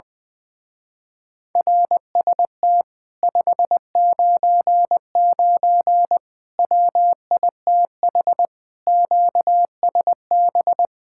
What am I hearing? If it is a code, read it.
Morse code, heard as a keyed tone, about 21 wpm, 700 Hz. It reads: RST 599 WITH QSB
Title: Speed-Racing Common QSO Phrases Encore - 20wpm